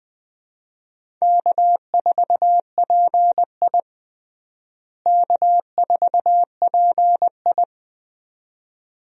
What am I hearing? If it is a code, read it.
Morse code, heard as a keyed tone, about 20 wpm, 700 Hz. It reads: K4PI K4PI
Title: USA Call Signs - 4 Characters - Repeated 2x - 20wpm